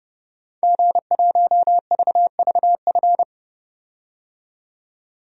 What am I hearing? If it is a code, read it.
Morse code, heard as a keyed tone, about 32 wpm, 700 Hz. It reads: G1VVF
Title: DXCC-100 Call Signs - 5 Characters - 30wpm